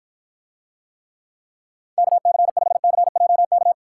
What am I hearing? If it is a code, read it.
Morse code, heard as a keyed tone, about 54 wpm, 700 Hz. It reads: KC5CJK